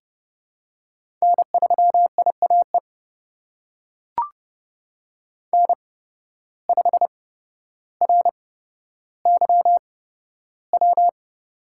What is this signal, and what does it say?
Morse code, heard as a keyed tone, about 30 wpm, spaced as for 13 wpm, 700 Hz. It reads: N3IAE N5RYW
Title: USA Call Signs - 5 Characters - 8x Character Spacing - 30wpm